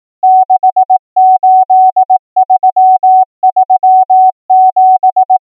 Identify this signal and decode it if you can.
Morse code, heard as a keyed tone, about 18 wpm, 750 Hz. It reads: 68337
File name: numbers / V2-morse